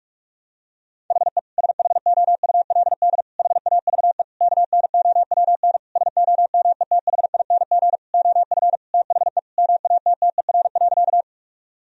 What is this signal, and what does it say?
Morse code, heard as a keyed tone, about 41 wpm, 700 Hz. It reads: HE SHOULD HAVE KNOWN SOMETHING OF THE MATTER.
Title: Sentences from Top 1000 Words - Review Part 3 - 45wpm